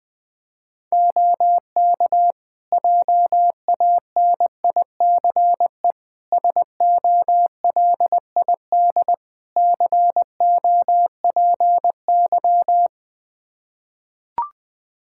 Morse code, heard as a keyed tone, about 21 wpm, 700 Hz. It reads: OK JANICE SOLID COPY
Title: K1USN SST QSO Practice 20wpm